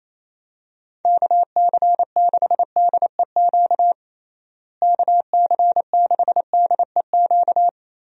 Morse code, heard as a keyed tone, about 27 wpm, 700 Hz. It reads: KC6DEQ KC6DEQ
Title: USA Call Signs - 6 Characters - Repeated 2x - 28wpm